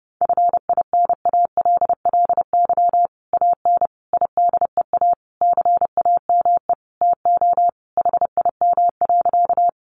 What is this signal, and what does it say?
Morse code, heard as a keyed tone, about 28 wpm, 700 Hz. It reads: FINALLY AN IDEA CAME TO HIM.
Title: Sentences from Top 1000 Words 30wpm